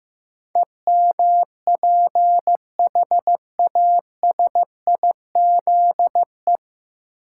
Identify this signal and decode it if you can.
Morse code, heard as a keyed tone, about 15 wpm, 700 Hz. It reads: EMPHASIZE
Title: Sentences from Top 1500 Words - New Word Drill Alt - 15wpm